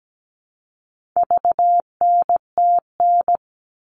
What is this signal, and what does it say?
Morse code, heard as a keyed tone, about 17 wpm, 700 Hz. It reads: VNTN
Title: All Cut Numbers - 4-Digit 17wpm